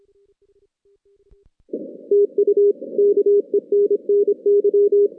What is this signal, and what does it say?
Morse code, heard as a keyed tone, about 26 wpm, 410 Hz. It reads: TU KENNY
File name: 2016 JAN CW NAQP / 2016 Jan naqp k5ot - 0-8